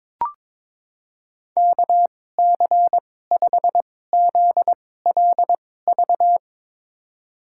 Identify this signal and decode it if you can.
Morse code, heard as a keyed tone, about 22 wpm, spaced as for 16 wpm, 700 Hz. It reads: KC5ZLV
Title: USA Call Signs - 6 Characters - 2x Character Spacing - 22wpm